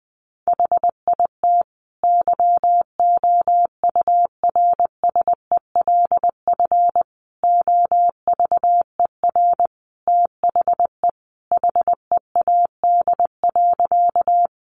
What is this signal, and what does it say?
Morse code, heard as a keyed tone, about 20 wpm, 700 Hz. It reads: HIT YOURSELF OVER THE HEAD.
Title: Sentences from Top 1200 Words 20wpm